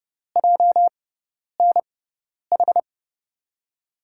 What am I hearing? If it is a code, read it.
Morse code, heard as a keyed tone, about 30 wpm, spaced as for 10 wpm, 700 Hz. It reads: JNH
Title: USA Call Signs - 6 Characters - 6x Character Spacing - 30wpm